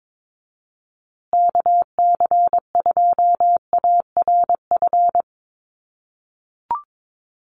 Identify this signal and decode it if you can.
Morse code, heard as a keyed tone, about 22 wpm, 700 Hz. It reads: KC2ARF
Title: USA Call Signs Repeated 2x 22wpm